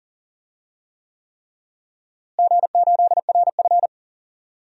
Morse code, heard as a keyed tone, about 37 wpm, 700 Hz. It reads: G8RF
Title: DXCC-100 Call Signs - 4 Characters - Repeated 3x - 40wpm